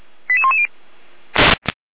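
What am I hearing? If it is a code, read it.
Five-tone selective call (ZVEI), heard as quick runs of five tones, tones of about 70 ms. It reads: 88110
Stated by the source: SDR-received ZVEI five-tone selective call, code 88103, recorded by feuerwehr-orlamuende.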